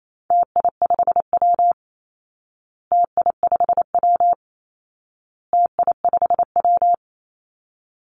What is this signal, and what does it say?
Morse code, heard as a keyed tone, about 28 wpm, 700 Hz. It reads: TI5W TI5W TI5W
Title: Call Signs Repeated 2x - 28wpm